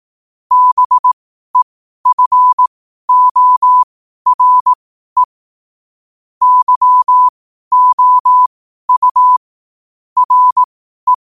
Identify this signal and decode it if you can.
Morse code, heard as a keyed tone, about 18 wpm, spaced as for 12 wpm, 1000 Hz. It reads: BEFORE YOU RE
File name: qotdfiles / 12wpm